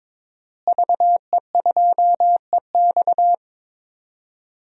Morse code, heard as a keyed tone, about 22 wpm, 700 Hz. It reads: VE2EX
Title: DXCC-100 Call Signs - 5 Characters - Repeated 2x - 22wpm